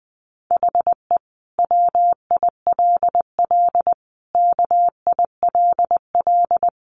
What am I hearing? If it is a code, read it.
Morse code, heard as a keyed tone, about 20 wpm, 700 Hz. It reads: HE WILL KILL
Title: Sentences from Top 1400 Words 20wpm